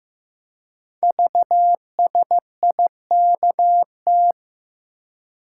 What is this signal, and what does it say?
Morse code, heard as a keyed tone, about 15 wpm, 700 Hz. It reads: VSIKT
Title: Code Groups - 5 Letters - 15wpm